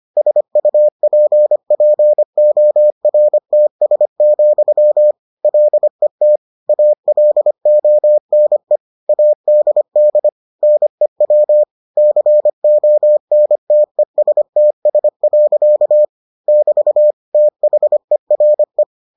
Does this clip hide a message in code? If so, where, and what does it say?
Yes: Morse code from 0.2 to 18.8 s, heard as a keyed tone, about 25 wpm, 600 Hz. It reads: SUPPORTS, LET ALONE ADD NEW CONTESTS. <BT> THERE